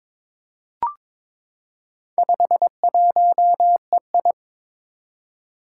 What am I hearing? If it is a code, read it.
Morse code, heard as a keyed tone, about 22 wpm, 700 Hz. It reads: 51EI
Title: Four Challenging Letter-Number Combinations 22wpm